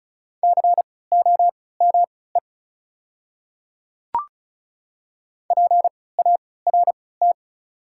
Morse code, heard as a keyed tone, about 36 wpm, spaced as for 20 wpm, 700 Hz. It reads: COME PART
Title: 3x Character Spacing - Top 200 Words - 35wpm